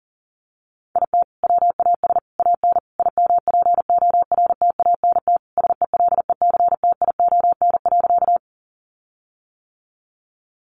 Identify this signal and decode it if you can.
Morse code, heard as a keyed tone, about 40 wpm, 700 Hz. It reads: IT WAS AN IMPORTANT SELECTION.
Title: Sentences from Top 1400 Words 40wpm